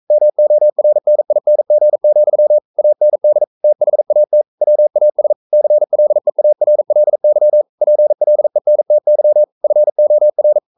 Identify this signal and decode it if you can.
Morse code, heard as a keyed tone, about 42 wpm, 600 Hz. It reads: MORNING, AND THAT WAS CLEARLY PLENTY FOR
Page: Romanen Never af Ken Follet på engelsk.